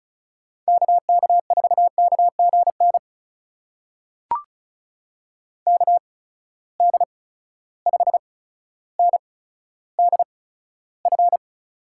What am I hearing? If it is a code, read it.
Morse code, heard as a keyed tone, about 36 wpm, spaced as for 15 wpm, 700 Hz. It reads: KK4KGN KD5NDF
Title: USA Call Signs - 6 Characters - 8x Character Spacing - 35wpm